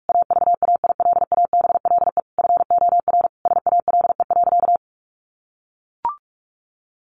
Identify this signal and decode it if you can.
Morse code, heard as a keyed tone, about 44 wpm, 700 Hz. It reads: AVAILABLE FOR SALE.
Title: Sentences from Top 1400 Words 45wpm